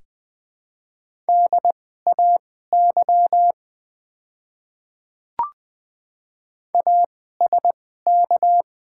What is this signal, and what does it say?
Morse code, heard as a keyed tone, about 20 wpm, spaced as for 15 wpm, 700 Hz. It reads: DAY ASK
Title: 2x Character Spacing - Top 200 Words - 20wpm